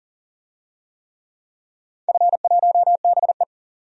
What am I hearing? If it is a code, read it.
Morse code, heard as a keyed tone, about 37 wpm, 700 Hz. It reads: F1BE
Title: DXCC-100 Call Signs - 4 Characters - 40wpm